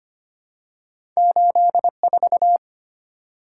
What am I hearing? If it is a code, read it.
Morse code, heard as a keyed tone, about 25 wpm, 700 Hz. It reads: 84
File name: Two Letter-Number Combinations - 25wpm